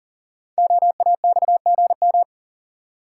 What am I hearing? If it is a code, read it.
Morse code, heard as a keyed tone, about 41 wpm, 700 Hz. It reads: OAXGM
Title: Code Groups - 5 Letters - 40wpm